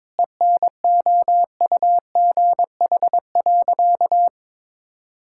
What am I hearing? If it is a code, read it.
Morse code, heard as a keyed tone, about 22 wpm, 700 Hz. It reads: ENOUGH.
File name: practice / 22wpm / Sentences with 4-Word Phrases - 22wpm